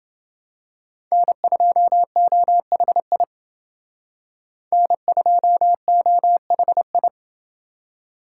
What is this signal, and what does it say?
Morse code, heard as a keyed tone, about 29 wpm, 700 Hz. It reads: N2OHI N2OHI